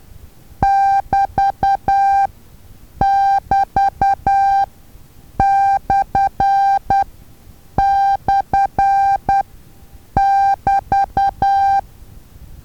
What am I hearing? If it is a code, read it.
Morse code, heard as a keyed tone, about 10 wpm, spaced as for 7 wpm, 790 Hz. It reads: <BT><BT>//<BT>
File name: Taviro HA6VA A